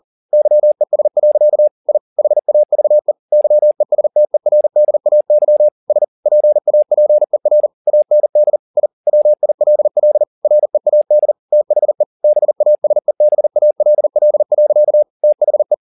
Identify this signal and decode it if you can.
Morse code, heard as a keyed tone, about 40 wpm, 600 Hz. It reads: YES. I HAVE YESTERDAY S PAPER AND I WILL READ THE BASEBALL. THE